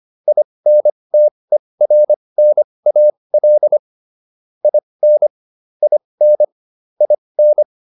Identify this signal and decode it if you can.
Morse code, heard as a keyed tone, about 25 wpm, spaced as for 20 wpm, 600 Hz. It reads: INTERNAL IN IN IN
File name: IN8 20